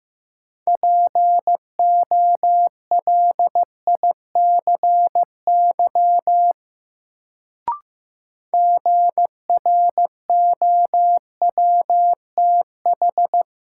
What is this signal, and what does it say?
Morse code, heard as a keyed tone, about 15 wpm, 700 Hz. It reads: POLICY GROWTH
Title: Sentences from Top 1000 Words - Review Part 5 - New Word Drill - 15wpm